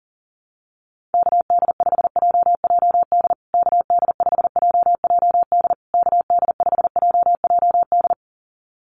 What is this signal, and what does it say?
Morse code, heard as a keyed tone, about 40 wpm, 700 Hz. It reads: KD5JJD KD5JJD KD5JJD
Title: USA Call Signs Repeated 3x 40wpm